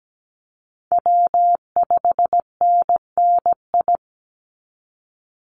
Morse code, heard as a keyed tone, about 17 wpm, 700 Hz. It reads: W5NNI
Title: USA Call Signs 17wpm